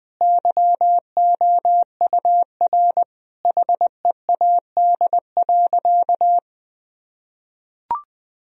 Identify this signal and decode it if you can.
Morse code, heard as a keyed tone, about 20 wpm, 700 Hz. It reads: YOUR HEAD.
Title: Sentences from Top 1000 Words - Review Part 1 - 20wpm